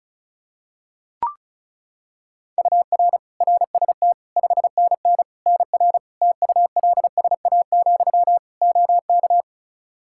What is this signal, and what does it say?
Morse code, heard as a keyed tone, about 36 wpm, 700 Hz. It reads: UR RST 5NN NR TULSA, OK
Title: Practice QSO - Exchange 1 - Beginning - 35wpm